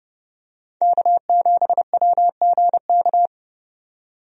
Morse code, heard as a keyed tone, about 32 wpm, 700 Hz. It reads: K7WGK
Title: USA Call Signs - 5 Characters - 30wpm